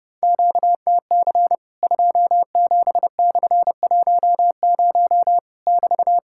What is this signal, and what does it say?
Morse code, heard as a keyed tone, about 30 wpm, 700 Hz. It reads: QTC 27/10 <BT>
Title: Worked All Europe DX Contest - QTC Exchange - 30wpm